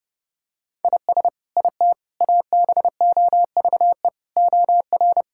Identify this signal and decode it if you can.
Morse code, heard as a keyed tone, about 30 wpm, 700 Hz. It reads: IS IT ABOVE OR
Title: Sentences from Top 1000 Words - Review Part 5 - 30wpm